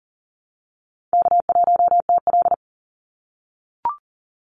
Morse code, heard as a keyed tone, about 40 wpm, 700 Hz. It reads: K1TL
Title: Easy Call Signs - 40wpm